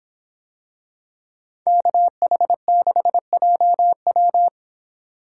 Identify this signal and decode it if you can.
Morse code, heard as a keyed tone, about 26 wpm, 700 Hz. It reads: KH6JW